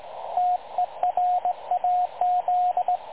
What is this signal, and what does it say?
Morse code, heard as a keyed tone, about 19 wpm, 720 Hz, with radio noise. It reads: TERAZ
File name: dlazaawansowanych / nagraniazpasma / websdr recording 3565.0kHz